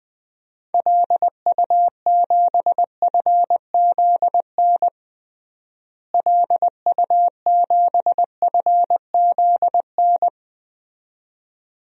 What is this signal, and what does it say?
Morse code, heard as a keyed tone, about 20 wpm, 700 Hz. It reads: LU7FZN LU7FZN
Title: DXCC-100 Call Signs - 6 Characters - Repeated 2x - 20wpm